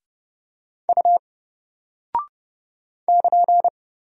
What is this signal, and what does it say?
Morse code, heard as a keyed tone, about 30 wpm, 700 Hz. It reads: U <KN>
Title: Single Character (Letter:Number:Punctuation:Prosign) 30wpm